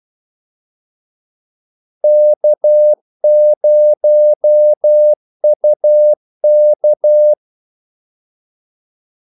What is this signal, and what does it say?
Morse code, heard as a keyed tone, about 12 wpm, 600 Hz. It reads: K0UK
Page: 2 ekstra ordmellemrum mellem kaldesignalerne